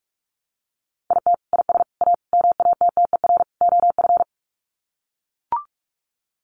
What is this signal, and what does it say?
Morse code, heard as a keyed tone, about 44 wpm, 700 Hz. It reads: IT IS A MATTER OF
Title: Sets of 5 Words Encore2 45wpm